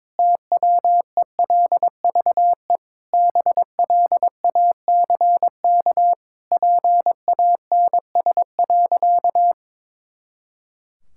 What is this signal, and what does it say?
Morse code, heard as a keyed tone, about 22 wpm, 700 Hz. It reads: TWELVE BLACK PANS.
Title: Pangram Challenge v2 - 22wpm